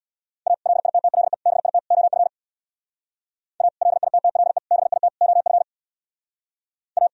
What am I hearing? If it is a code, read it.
Morse code, heard as a keyed tone, about 64 wpm, 700 Hz. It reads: A LITTLE BIT OF A LITTLE BIT OF A